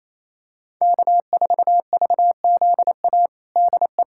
Morse code, heard as a keyed tone, about 27 wpm, 700 Hz. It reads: K4VZA DE
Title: USA Callsign DE Callsign - 5 Characters - 28wpm